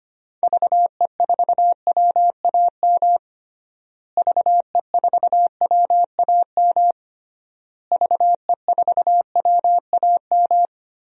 Morse code, heard as a keyed tone, about 25 wpm, 700 Hz. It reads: VE4WAM VE4WAM VE4WAM
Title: DXCC-100 Call Signs - 6 Characters - Repeated 3x - 25wpm